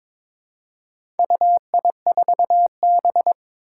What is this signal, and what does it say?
Morse code, heard as a keyed tone, about 22 wpm, 700 Hz. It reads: UI4B
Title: Four Challenging Letter-Number Combinations 22wpm